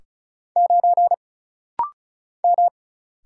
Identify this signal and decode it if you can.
Morse code, heard as a keyed tone, about 35 wpm, 700 Hz. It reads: 9 M
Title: Single Letter-Number Alt - Rapid-Fire 35wpm